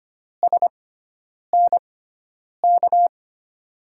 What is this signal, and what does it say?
Morse code, heard as a keyed tone, about 25 wpm, spaced as for 9 wpm, 700 Hz. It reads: SNK
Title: USA Call Signs - 6x Character Spacing - 25wpm